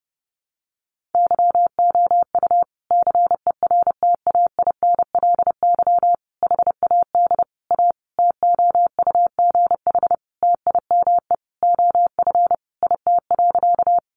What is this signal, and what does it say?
Morse code, heard as a keyed tone, about 30 wpm, 700 Hz. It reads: YOU CERTAINLY HAD A TOUGH TIME OF IT.
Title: Sentences from Top 1200 Words 30wpm